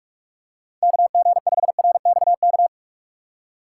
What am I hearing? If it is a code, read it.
Morse code, heard as a keyed tone, about 46 wpm, 700 Hz. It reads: KG5RXK
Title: USA Call Signs - 3x Character Spacing - 45wpm